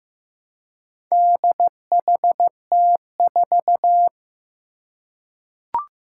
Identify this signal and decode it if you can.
Morse code, heard as a keyed tone, about 15 wpm, 700 Hz. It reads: DHT4
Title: Four Challenging Letter-Number Combinations 15wpm